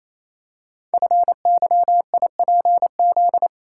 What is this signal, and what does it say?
Morse code, heard as a keyed tone, about 28 wpm, 700 Hz. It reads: FYIPZ
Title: Code Groups - 5 Letter:Numbers - 28wpm